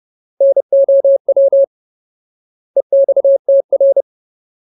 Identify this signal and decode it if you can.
Morse code, heard as a keyed tone, about 30 wpm, 550 Hz. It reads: NOW EXTR
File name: Ultimate Word List / 30WPM Ultimate